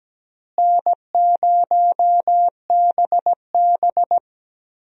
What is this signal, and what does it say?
Morse code, heard as a keyed tone, about 17 wpm, 700 Hz. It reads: N0BB
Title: USA Call Signs - 4 Characters - Repeated 3x - 17wpm